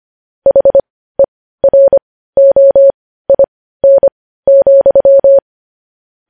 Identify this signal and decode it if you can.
Morse code, heard as a keyed tone, about 25 wpm, spaced as for 15 wpm, 550 Hz. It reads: HEROIN,
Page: news headlines in morse code at 15 wpm